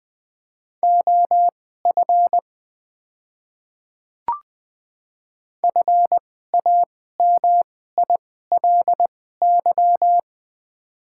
Morse code, heard as a keyed tone, about 20 wpm, spaced as for 14 wpm, 690 Hz. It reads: OF FAMILY
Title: 2x Character Spacing - Top 500 Words - 20wpm